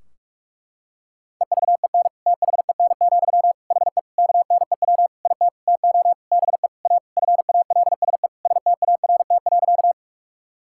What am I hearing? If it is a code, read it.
Morse code, heard as a keyed tone, about 46 wpm, 700 Hz. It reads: EVEN THEN, HE KNEW IT TO BE A FALSE START.
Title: Sentences from Top 1600 Words 45wpm